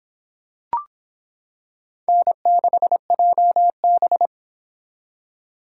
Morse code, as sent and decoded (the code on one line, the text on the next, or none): -. -.... .--- -...
N6JB